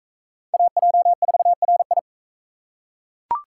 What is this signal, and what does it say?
Morse code, heard as a keyed tone, about 41 wpm, 700 Hz. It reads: AJ4RI